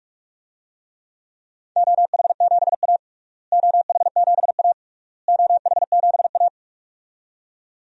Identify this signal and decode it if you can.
Morse code, heard as a keyed tone, about 46 wpm, 700 Hz. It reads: OH7A OH7A OH7A
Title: DXCC-100 Call Signs - 4 Characters - Repeated 3x - 45wpm